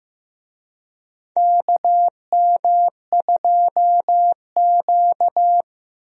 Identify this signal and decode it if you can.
Morse code, heard as a keyed tone, about 15 wpm, 700 Hz. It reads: KM2Q
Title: USA Call Signs - 4 Characters - 15wpm